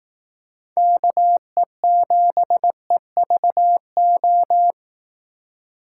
Morse code, heard as a keyed tone, about 18 wpm, 700 Hz. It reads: KE7EVO
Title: Speed Racing - USA 2x3 Call Signs 15wpm